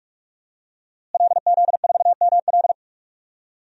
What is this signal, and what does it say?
Morse code, heard as a keyed tone, about 45 wpm, 700 Hz. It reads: LZ4ML
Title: DXCC-100 Call Signs - 5 Characters - Repeated 2x - 45wpm